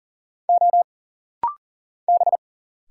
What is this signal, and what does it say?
Morse code, heard as a keyed tone, about 40 wpm, 700 Hz. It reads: O B
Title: Single Letter-Number Alt - Rapid-Fire 40wpm